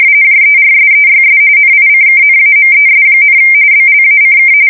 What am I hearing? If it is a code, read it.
RTTY, about 45.45 baud. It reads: JERSEYS NICKNAMES, BLACK
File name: fsk170 / USS-New-Jersey